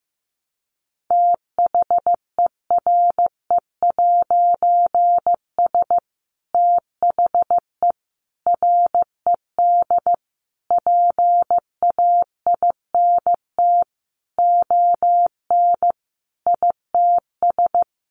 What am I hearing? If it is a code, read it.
Morse code, heard as a keyed tone, about 15 wpm, 700 Hz. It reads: THERE'S THE RED PAINT ON ITS
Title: A Sound Of Thunder 15wpm